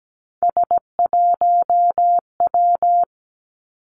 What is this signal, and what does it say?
Morse code, heard as a keyed tone, about 17 wpm, 700 Hz. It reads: S1W
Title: Three Challenging Letter-Number Combinations 17wpm